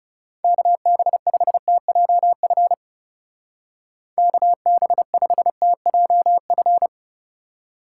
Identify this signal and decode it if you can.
Morse code, heard as a keyed tone, about 35 wpm, 700 Hz. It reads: KB5TJF KB5TJF
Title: Speed Racing - USA 2x3 Call Signs 30wpm